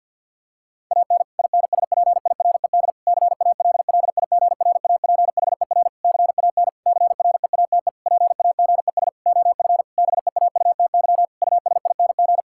Morse code, heard as a keyed tone, about 50 wpm, 700 Hz. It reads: AN INSPIRED CALLIGRAPHER CAN CREATE PAGES OF BEAUTY USING
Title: Pangram Challenge v2 - 50wpm